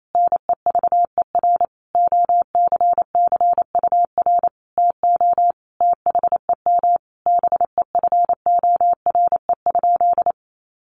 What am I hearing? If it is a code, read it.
Morse code, heard as a keyed tone, about 28 wpm, 700 Hz. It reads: NEVER OCCUR TO THEM BEFORE?
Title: Sentences from Top 900 Words 28wpm